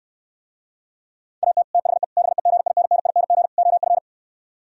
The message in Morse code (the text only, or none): AT THE BEGINNING OF